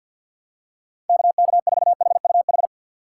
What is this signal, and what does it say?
Morse code, heard as a keyed tone, about 52 wpm, 700 Hz. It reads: KK4HUH
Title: USA Call Signs - 6 Characters - 6x Character Spacing - 50wpm